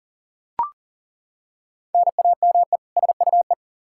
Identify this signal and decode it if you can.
Morse code, heard as a keyed tone, about 40 wpm, 700 Hz. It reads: NAME SUE
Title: Practice QSO - Exchange 1 - Beginning - 40wpm